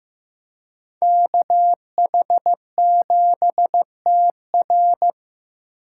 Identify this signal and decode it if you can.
Morse code, heard as a keyed tone, about 18 wpm, 700 Hz. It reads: KH7TR
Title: Speed Racing - USA 2x2 Call Signs 15wpm